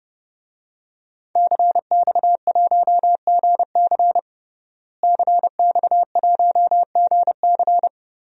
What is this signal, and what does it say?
Morse code, heard as a keyed tone, about 30 wpm, 700 Hz. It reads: CX1GC CX1GC
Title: DXCC-100 Call Signs - 5 Characters - Repeated 3x - 30wpm